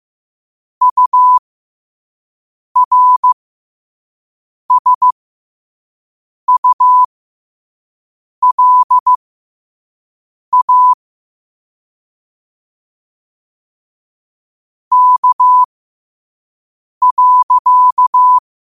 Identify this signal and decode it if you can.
Morse code, heard as a keyed tone, about 15 wpm, spaced as for 5 wpm, 1000 Hz. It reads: URSULA K.
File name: qotdfiles / 5wpm